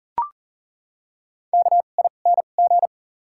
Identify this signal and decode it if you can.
Morse code, heard as a keyed tone, about 40 wpm, spaced as for 29 wpm, 700 Hz. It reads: KING
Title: 2x Character Spacing - Top 500 Words - 40wpm